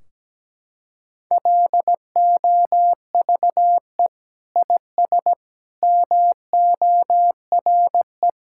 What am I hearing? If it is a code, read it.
Morse code, heard as a keyed tone, about 17 wpm, 700 Hz. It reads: LOVE IS MORE
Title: Sentences from Top 1600 Words 17wpm